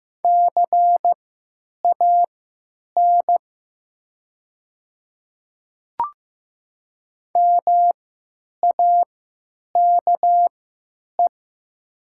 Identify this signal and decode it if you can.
Morse code, heard as a keyed tone, about 15 wpm, spaced as for 9 wpm, 700 Hz. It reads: CAN MAKE
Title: 3x Character Spacing - Top 100 Words - 15wpm